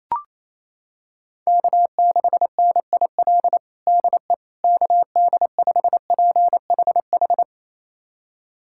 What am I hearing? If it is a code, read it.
Morse code, heard as a keyed tone, about 28 wpm, 700 Hz. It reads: K6NIL DE KD5PHH
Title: USA Callsign DE Callsign 28wpm